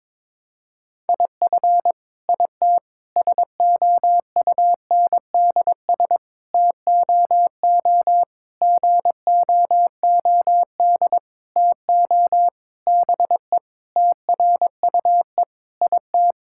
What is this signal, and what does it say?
Morse code, heard as a keyed tone, about 22 wpm, 700 Hz. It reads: IF IT SOUNDS TOO GOOD TO BE TRUE IT
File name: Sentences with 9-Word Phrases - 22wpm